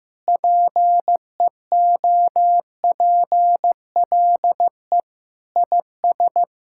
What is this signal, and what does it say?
Morse code, heard as a keyed tone, about 15 wpm, 700 Hz. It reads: PEOPLE IS
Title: Sentences from Top 1000 Words - Review Part 6 - 15wpm